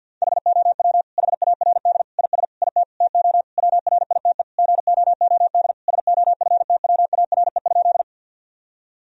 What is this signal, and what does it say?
Morse code, heard as a keyed tone, about 50 wpm, 700 Hz. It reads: HOW HARD IS IT TO WRITE GOOD SOFTWARE?
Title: Sentences from Top 1000 Words - Review Part 3 - 50wpm